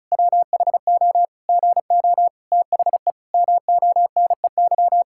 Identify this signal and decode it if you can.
Morse code, heard as a keyed tone, about 36 wpm, 700 Hz. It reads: WHO GO THE MONEY
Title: Pangram Challenge v2 - 35wpm